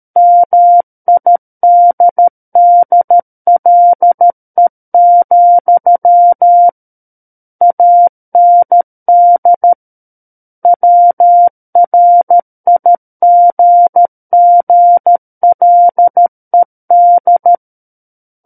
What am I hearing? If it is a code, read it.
Morse code, heard as a keyed tone, about 13 wpm, 700 Hz. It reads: MIDDLE, AND WRIGGLED
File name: Book One / 04-Chapter 4 - 13 WPM